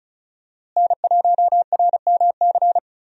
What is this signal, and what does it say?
Morse code, heard as a keyed tone, about 36 wpm, 700 Hz. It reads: N1RMC
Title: USA Call Signs - 5 Characters - Repeated 3x - 35wpm